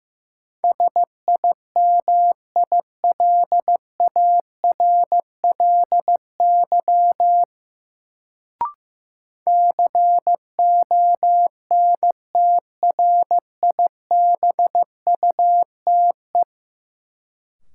Morse code, heard as a keyed tone, about 15 wpm, 700 Hz. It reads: SIMILARLY CONTRIBUTE
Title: Sentences from Top 1400 Words - New Word Drill - 15wpm